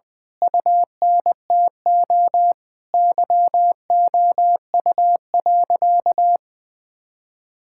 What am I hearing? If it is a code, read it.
Morse code, heard as a keyed tone, about 20 wpm, 700 Hz. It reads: UNTO YOU.
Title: Sentences with 8-Word Phrases - Spoken Beginning - 20wpm